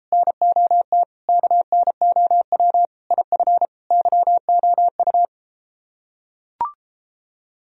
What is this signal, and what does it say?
Morse code, heard as a keyed tone, about 33 wpm, 700 Hz. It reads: NOT KNOW IF YOU